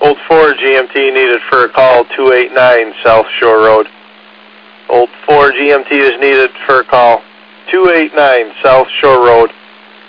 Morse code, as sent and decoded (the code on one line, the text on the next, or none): none